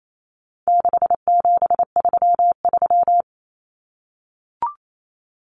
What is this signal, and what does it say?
Morse code, heard as a keyed tone, about 28 wpm, 700 Hz. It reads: 6733
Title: Numbers - 4-Digit 28wpm